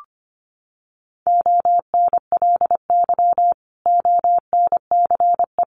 Morse code, heard as a keyed tone, about 25 wpm, 700 Hz. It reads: ONLY ONCE
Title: Sentences from Top 1100 Words 25wpm